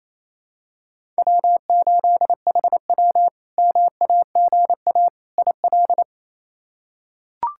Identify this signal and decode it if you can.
Morse code, heard as a keyed tone, about 28 wpm, 700 Hz. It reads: W8HW MAGA IL
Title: K1USN SST QSO Practice - 28wpm